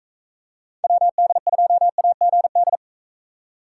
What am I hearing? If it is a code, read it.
Morse code, heard as a keyed tone, about 44 wpm, 700 Hz. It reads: WD2AGD